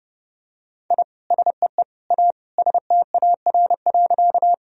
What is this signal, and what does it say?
Morse code, heard as a keyed tone, about 30 wpm, 700 Hz. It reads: I SEE A STAR.
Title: Sentences from Top 1000 Words - Review Part 2 - 30wpm